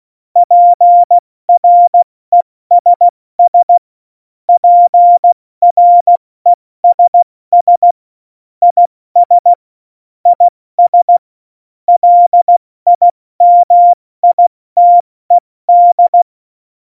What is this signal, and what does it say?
Morse code, heard as a keyed tone, about 16 wpm, 700 Hz. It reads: PRESS PRESS IS IS LIMITED
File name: Morsekurs / Witze / Witz 25 Juli 17, Tempo 70 F80